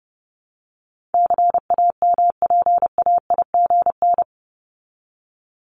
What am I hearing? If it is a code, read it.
Morse code, heard as a keyed tone, about 30 wpm, 700 Hz. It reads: CAMPAIGN
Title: Top 1001-1250 Words 30wpm